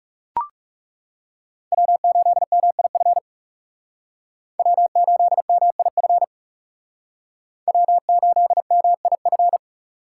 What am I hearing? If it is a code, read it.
Morse code, heard as a keyed tone, about 39 wpm, 700 Hz. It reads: W8MIF W8MIF W8MIF